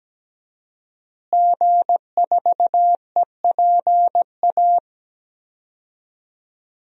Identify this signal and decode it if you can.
Morse code, heard as a keyed tone, about 17 wpm, 700 Hz. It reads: G4EPA
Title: DXCC-100 Call Signs - 5 Characters - Repeated 2x - 17wpm